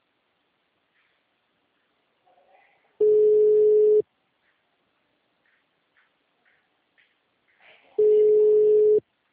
Ringback tone (Europe, 425 Hz).